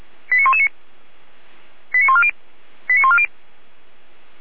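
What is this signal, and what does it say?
Five-tone selective call (ZVEI), heard as quick runs of five tones, tones of about 70 ms. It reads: 89119 89140 89140
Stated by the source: ZVEI five-tone selective call 89115 received via SDR.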